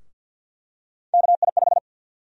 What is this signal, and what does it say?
Morse code, heard as a keyed tone, about 49 wpm, 700 Hz. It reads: KI5
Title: USA Call Sign Prefixes - 50wpm